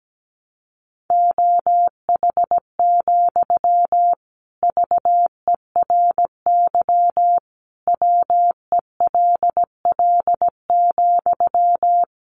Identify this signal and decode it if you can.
Morse code, heard as a keyed tone, about 17 wpm, 700 Hz. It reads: OH, VERY WELL,
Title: Sentences from Top 500 Words 17wpm